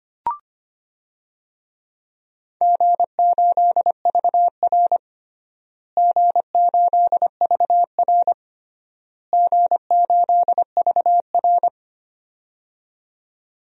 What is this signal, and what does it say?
Morse code, heard as a keyed tone, about 25 wpm, 700 Hz. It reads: G8VR G8VR G8VR
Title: DXCC-100 Call Signs - 4 Characters - Repeated 3x - 25wpm